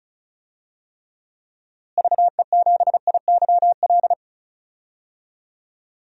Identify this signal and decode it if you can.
Morse code, heard as a keyed tone, about 34 wpm, 700 Hz. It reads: VE7IYL
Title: DXCC-100 Call Signs - 6 Characters - Repeated 3x - 35wpm